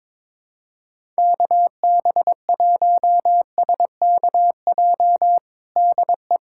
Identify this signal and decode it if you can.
Morse code, heard as a keyed tone, about 21 wpm, 700 Hz. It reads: KB1SKJ DE
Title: USA Callsign DE Callsign 22wpm